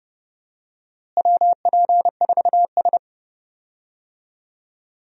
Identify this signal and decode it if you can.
Morse code, heard as a keyed tone, about 29 wpm, 700 Hz. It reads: WP4S
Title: USA Call Signs - 4 Characters - Repeated 3x - 30wpm